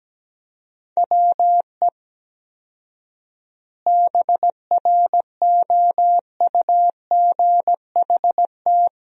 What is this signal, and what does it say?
Morse code, heard as a keyed tone, about 17 wpm, 700 Hz. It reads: WE BROUGHT
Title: 4x Word-Spacing - Sentences from Top 500 Words - 17wpm